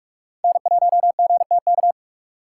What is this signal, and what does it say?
Morse code, heard as a keyed tone, about 40 wpm, 700 Hz. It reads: N1GTK